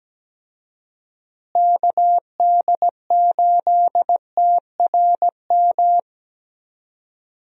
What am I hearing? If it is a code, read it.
Morse code, heard as a keyed tone, about 17 wpm, 700 Hz. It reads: KD8TRM
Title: USA Call Signs - 6 Characters - Repeated 2x - 17wpm